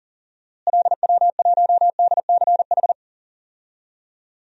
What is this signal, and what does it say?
Morse code, heard as a keyed tone, about 40 wpm, 700 Hz. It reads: LW1DCH
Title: DXCC-100 Call Signs - 6 Characters - Repeated 2x - 40wpm